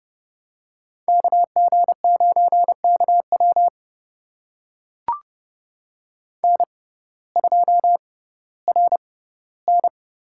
Morse code, heard as a keyed tone, about 31 wpm, spaced as for 17 wpm, 700 Hz. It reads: KG9KW N2RN
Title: USA Call Signs - 6x Character Spacing - 30wpm